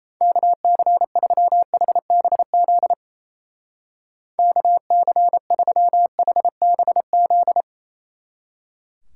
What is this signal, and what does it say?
Morse code, heard as a keyed tone, about 34 wpm, 700 Hz. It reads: KC3HBZ KC3HBZ
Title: Speed Racing - USA Call Signs 28wpm